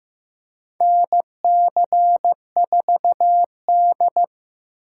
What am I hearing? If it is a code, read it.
Morse code, heard as a keyed tone, about 15 wpm, 700 Hz. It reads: NC4D
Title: USA Call Signs - 4 Characters - 8x Character Spacing - 15wpm